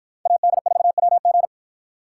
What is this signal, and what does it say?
Morse code, heard as a keyed tone, about 53 wpm, 700 Hz. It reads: AD4WG